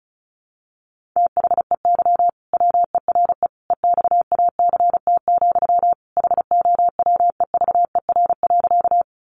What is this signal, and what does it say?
Morse code, heard as a keyed tone, about 34 wpm, 700 Hz. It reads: THEY WERE EXACT, HOWEVER.
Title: Sentences from Top 1300 Words 35wpm